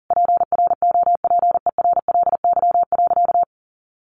Morse code, heard as a keyed tone, about 40 wpm, 700 Hz. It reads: PROPERLY.
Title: Sentences from Top 1100 Words 40wpm